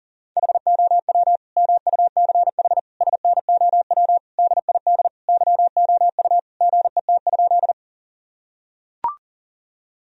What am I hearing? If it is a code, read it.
Morse code, heard as a keyed tone, about 40 wpm, 700 Hz. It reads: HOW MUCH SNOW DID YOU GET?
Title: Sentences from Top 1000 Words - Review Part 3 - 40wpm